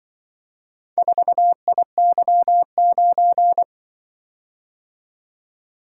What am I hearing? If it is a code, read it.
Morse code, heard as a keyed tone, about 24 wpm, 700 Hz. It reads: 4IY9
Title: Speed-Racing - Four Letter-Number Combinations - 20wpm